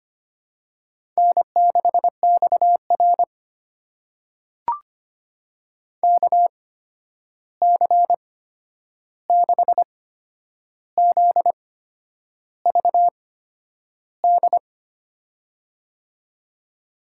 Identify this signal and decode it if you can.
Morse code, heard as a keyed tone, about 25 wpm, spaced as for 9 wpm, 700 Hz. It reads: N6XR KC6ZVD
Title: USA Call Signs - 8x Character Spacing - 25wpm